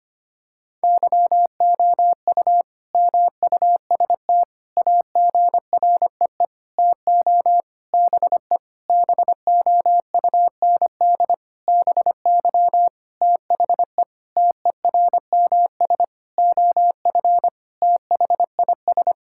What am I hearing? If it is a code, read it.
Morse code, heard as a keyed tone, about 25 wpm, 700 Hz. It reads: YOU MUST AGREE TO BE BOUND BY THE TERMS OF THIS
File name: practice / 25wpm / Sentences with 9-Word Phrases - 25wpm